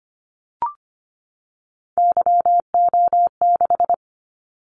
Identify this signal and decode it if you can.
Morse code, heard as a keyed tone, about 25 wpm, 700 Hz. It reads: YO6
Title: Callsign Prefixes for Top 25 DXCC - 25wpm